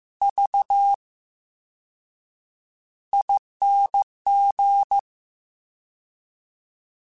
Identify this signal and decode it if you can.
Morse code, heard as a keyed tone, about 15 wpm, spaced as for 7 wpm, 800 Hz. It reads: VING
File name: Special INGs-mp3s / Special INGs-mp3s / VerySlowToFaster Ing @13